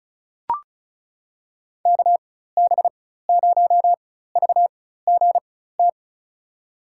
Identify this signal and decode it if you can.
Morse code, heard as a keyed tone, about 36 wpm, spaced as for 16 wpm, 700 Hz. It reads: KB0VGT
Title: USA Call Signs - 4x Character Spacing - 35wpm